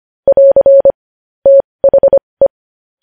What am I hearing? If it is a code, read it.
Morse code, heard as a keyed tone, about 25 wpm, spaced as for 20 wpm, 550 Hz. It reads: <AR> THE
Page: news-headlines-in-morse-code-at-20-wpm